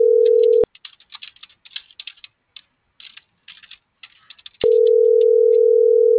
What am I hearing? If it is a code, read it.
Ringback tone (North America), 440 + 480 Hz.